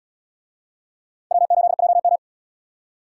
Morse code, heard as a keyed tone, about 70 wpm, 690 Hz. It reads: W8JM